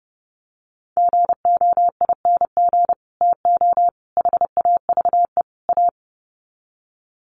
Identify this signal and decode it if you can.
Morse code, heard as a keyed tone, about 30 wpm, 700 Hz. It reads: GOING TO HAVE A
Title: Sets of 4 Words 30wpm